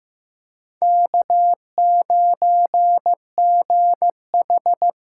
Morse code, heard as a keyed tone, about 15 wpm, 700 Hz. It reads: K9GH
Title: USA Call Signs - 5 Characters - 8x Character Spacing - 15wpm